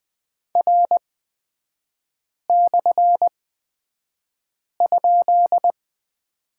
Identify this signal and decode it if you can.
Morse code, heard as a keyed tone, about 20 wpm, spaced as for 5 wpm, 700 Hz. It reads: R/?
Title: 250 Lesson 40e - BK - Three Characters 20wpm